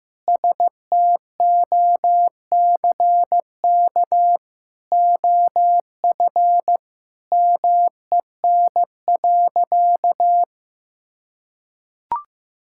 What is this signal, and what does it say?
Morse code, heard as a keyed tone, about 15 wpm, 700 Hz. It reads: STOCK OF MEN.
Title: Sentences from Top 1000 Words - Review Part 5 - 15wpm